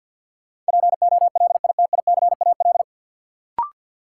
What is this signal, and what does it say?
Morse code, heard as a keyed tone, about 53 wpm, 700 Hz. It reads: POLITICAL
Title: Sentences from Top 800 Words - New Word Drill - 50wpm